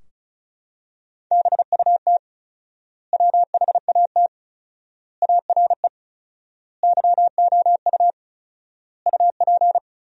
Morse code, heard as a keyed tone, about 36 wpm, 700 Hz. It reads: BUT WHAT ARE YOU UP
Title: 4x Word-Spacing - Sentences from Top 100 Words - 35wpm